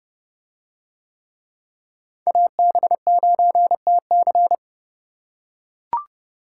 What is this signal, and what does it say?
Morse code, heard as a keyed tone, about 29 wpm, 700 Hz. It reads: AB9TC